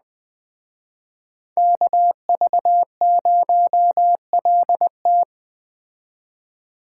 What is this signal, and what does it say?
Morse code, heard as a keyed tone, about 20 wpm, 700 Hz. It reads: KV0LT
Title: Speed Racing - USA Call Signs 17wpm